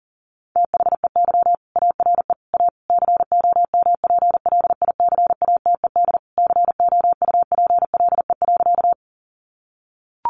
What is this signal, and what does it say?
Morse code, heard as a keyed tone, about 40 wpm, 700 Hz. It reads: THEY ARE A COMPLICATED COUPLE.
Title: Sentences from Top 1200 Words 40wpm